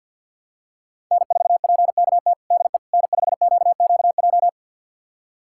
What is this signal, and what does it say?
Morse code, heard as a keyed tone, about 51 wpm, 700 Hz. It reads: N4PKT DE N5QQJ
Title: USA Callsign DE Callsign - 5 Characters - 50wpm